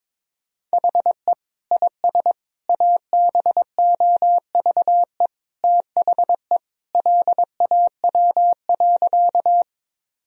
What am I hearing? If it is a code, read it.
Morse code, heard as a keyed tone, about 22 wpm, 700 Hz. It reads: HE IS ABOVE THE LAW.
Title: Sentences from Top 1000 Words - Review Part 3 - 22wpm